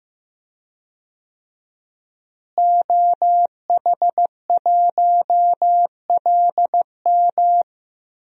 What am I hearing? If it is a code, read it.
Morse code, heard as a keyed tone, about 15 wpm, 700 Hz. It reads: OH1LM
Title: DXCC-100 Call Signs - 5 Characters - Repeated 2x - 15wpm